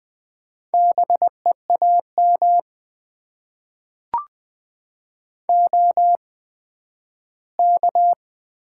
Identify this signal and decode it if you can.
Morse code, heard as a keyed tone, about 20 wpm, spaced as for 12 wpm, 700 Hz. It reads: BEAM OK
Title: Top 90 QSO Elements - 8x Character Spacing - 20wpm